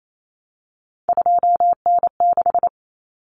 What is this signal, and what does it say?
Morse code, heard as a keyed tone, about 28 wpm, 700 Hz. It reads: 2N6
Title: Common Cut Numbers - 3-Digit 28wpm